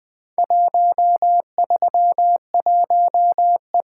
Morse code, heard as a keyed tone, about 20 wpm, 700 Hz. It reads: 131E
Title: Four Letter-Number Combinations - 20wpm